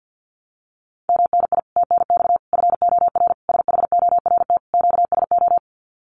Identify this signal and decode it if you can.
Morse code, heard as a keyed tone, about 48 wpm, 700 Hz. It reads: MNI TNX FOR SHORT QSO
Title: Common QSO Phrases 50wpm